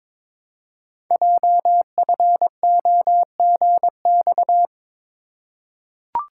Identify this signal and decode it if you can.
Morse code, heard as a keyed tone, about 23 wpm, 700 Hz. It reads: JFOGX
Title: Code Groups - 5 Letters - 22wpm